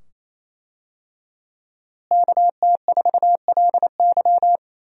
Morse code, heard as a keyed tone, about 28 wpm, 700 Hz. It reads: KT4LY
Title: USA Call Signs - 5 Characters - Repeated 2x - 28wpm